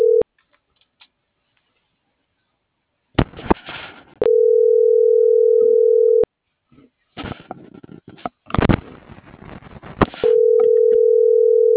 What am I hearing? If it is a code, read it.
Ringback tone (North America), 440 + 480 Hz.